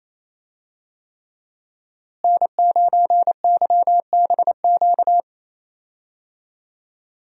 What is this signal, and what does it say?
Morse code, heard as a keyed tone, about 28 wpm, 700 Hz. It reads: N9YBQ
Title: USA Call Signs - 5 Characters - Repeated 3x - 28wpm